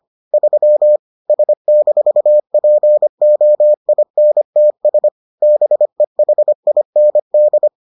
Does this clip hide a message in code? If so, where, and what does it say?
Yes: Morse code from 0.3 to 7.7 s, heard as a keyed tone, about 25 wpm, 600 Hz. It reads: 3 S-POINTS BEHIND